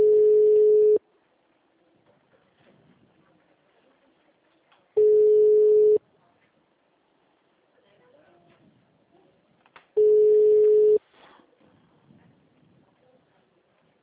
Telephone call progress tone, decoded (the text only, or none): ringback tone